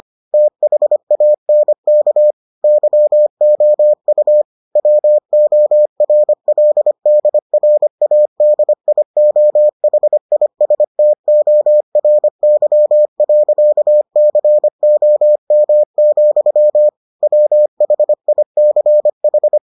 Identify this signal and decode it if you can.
Morse code, heard as a keyed tone, about 25 wpm, 600 Hz. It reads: THANK YOU WORLDRADIOHISTORY.COM, WHICH